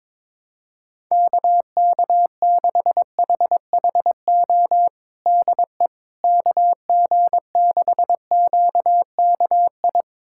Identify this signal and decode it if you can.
Morse code, heard as a keyed tone, about 22 wpm, 700 Hz. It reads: KK6HHO DE KG6QKI
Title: USA Callsign DE Callsign 22wpm